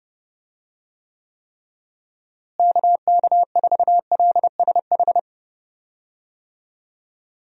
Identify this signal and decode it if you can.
Morse code, heard as a keyed tone, about 30 wpm, 700 Hz. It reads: KK4LSH
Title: USA Call Signs - 6 Characters - Repeated 3x - 30wpm